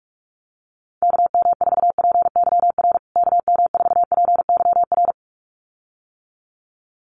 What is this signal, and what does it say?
Morse code, heard as a keyed tone, about 44 wpm, 700 Hz. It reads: KM4PYR KM4PYR
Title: USA Call Signs Repeated 2x 45wpm